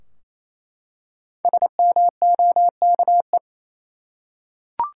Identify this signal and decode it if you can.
Morse code, heard as a keyed tone, about 28 wpm, 700 Hz. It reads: SMOKE
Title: Five Letter Words - 28wpm